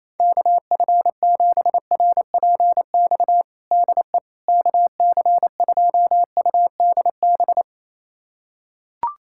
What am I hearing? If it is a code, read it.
Morse code, heard as a keyed tone, about 28 wpm, 700 Hz. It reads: KF7RPX DE KC2UDB
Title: USA Callsign DE Callsign - 6 Characters - 28wpm